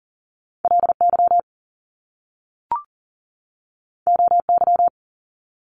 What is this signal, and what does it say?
Morse code, heard as a keyed tone, about 41 wpm, 700 Hz. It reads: LY OY
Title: Top Two Letter Combinations in Words 40wpm